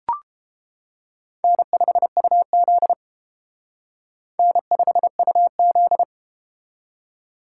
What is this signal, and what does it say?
Morse code, heard as a keyed tone, about 33 wpm, 700 Hz. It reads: N5UZ N5UZ